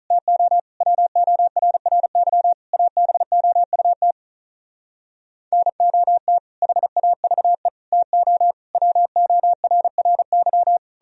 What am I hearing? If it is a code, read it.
Morse code, heard as a keyed tone, about 45 wpm, 700 Hz. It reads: TO WORRY ABOUT NOT HAVE TO WORRY